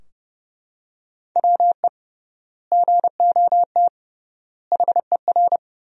Morse code, heard as a keyed tone, about 30 wpm, 700 Hz. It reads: WE GOT HER
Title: 3x Word-Spacing - Sentences from Top 400 Words - 30wpm